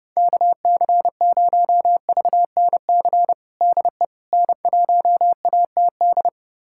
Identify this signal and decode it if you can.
Morse code, heard as a keyed tone, about 32 wpm, 700 Hz. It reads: KC0VNC DE N1ATD
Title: USA Callsign DE Callsign 30wpm